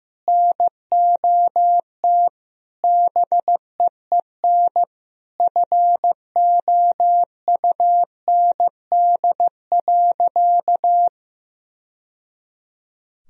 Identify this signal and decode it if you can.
Morse code, heard as a keyed tone, about 15 wpm, 700 Hz. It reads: NOT BEEN FOUND.
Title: Sentences from Top 1000 Words - Review Part 4 - 15wpm